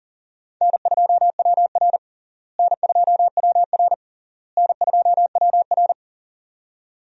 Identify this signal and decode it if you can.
Morse code, heard as a keyed tone, about 40 wpm, 700 Hz. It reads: N2WR N2WR N2WR
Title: USA Call Signs - 4 Characters - Repeated 3x - 40wpm